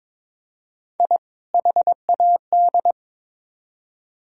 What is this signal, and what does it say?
Morse code, heard as a keyed tone, about 22 wpm, 700 Hz. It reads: I HAD
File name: Build A Phrase - 2-Word Phrases - 22wpm